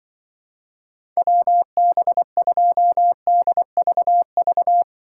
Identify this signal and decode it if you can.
Morse code, heard as a keyed tone, about 24 wpm, 700 Hz. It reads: WB2DVV
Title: Speed Racing - USA 2x3 Call Signs 20wpm